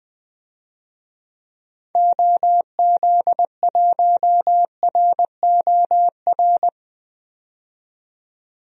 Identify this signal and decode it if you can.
Morse code, heard as a keyed tone, about 21 wpm, 700 Hz. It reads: OZ1ROR
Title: DXCC-100 Call Signs - 6 Characters - 20wpm